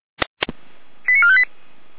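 Five-tone selective call (ZVEI), heard as quick runs of five tones, tones of about 70 ms. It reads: 89458
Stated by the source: ZVEI five-tone selective call 89323 received on SDR, from the Feuerwehr Orlamünde audio archive.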